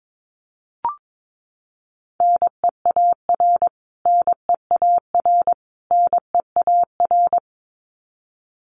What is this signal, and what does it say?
Morse code, heard as a keyed tone, about 22 wpm, 700 Hz. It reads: NEAR NEAR NEAR
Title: Top 500 Words Repeated 22wpm